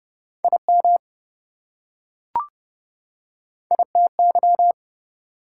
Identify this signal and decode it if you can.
Morse code, heard as a keyed tone, about 25 wpm, 700 Hz. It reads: IM ITY
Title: Speed-Racing Prefixes and Suffixes 20wpm